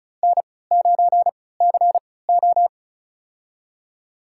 Morse code, heard as a keyed tone, about 36 wpm, spaced as for 20 wpm, 700 Hz. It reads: N9CO
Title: USA Call Signs - 4 Characters - 3x Character Spacing - 35wpm